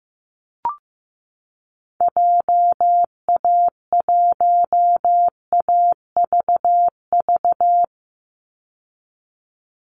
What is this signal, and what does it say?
Morse code, heard as a keyed tone, about 15 wpm, 700 Hz. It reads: JA1AVV
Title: USA Call Signs 15wpm